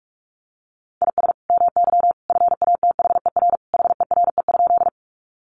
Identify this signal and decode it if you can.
Morse code, heard as a keyed tone, about 45 wpm, 700 Hz. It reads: IS MY FATHER HERE?
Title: Sentences from Top 300 Words 45wpm